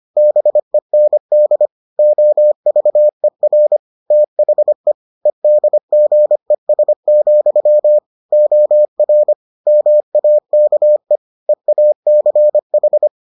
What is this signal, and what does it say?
Morse code, heard as a keyed tone, about 25 wpm, 600 Hz. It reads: BEND OVER THE EDGES, OR MAKE EACH